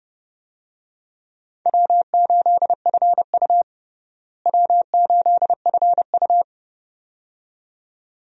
Morse code, heard as a keyed tone, about 30 wpm, 700 Hz. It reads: W8FU W8FU
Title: USA Call Signs - 4 Characters - Repeated 2x - 30wpm